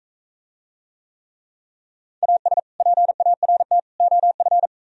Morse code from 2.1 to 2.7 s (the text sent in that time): AS